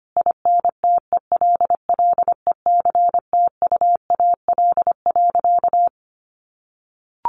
Morse code, heard as a keyed tone, about 25 wpm, 700 Hz. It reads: INTELLECTUAL.
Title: A Sound Of Thunder 25wpm